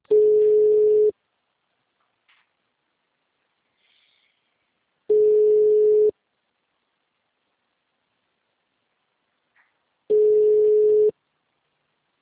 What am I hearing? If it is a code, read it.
Ringback tone (Europe, 425 Hz).
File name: out- -2002- - - .322